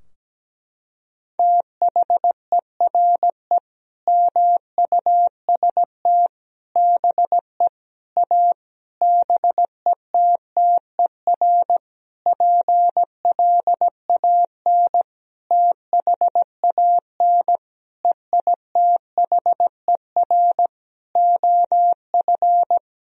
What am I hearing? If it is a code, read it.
Morse code, heard as a keyed tone, about 17 wpm, 700 Hz. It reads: THERE MUST BE A BETTER PLAN THAN EITHER OF
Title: Sentences from Top 1000 Words - Review Part 4 - 17wpm